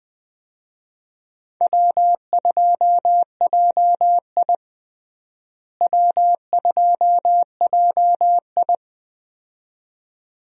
Morse code, heard as a keyed tone, about 20 wpm, 700 Hz. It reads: W2JI W2JI
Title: USA Call Signs - 4 Characters - Repeated 2x - 20wpm